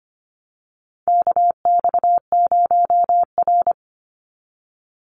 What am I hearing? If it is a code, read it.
Morse code, heard as a keyed tone, about 25 wpm, 700 Hz. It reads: KX0R
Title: NA SOTA Activator Callsigns 25wpm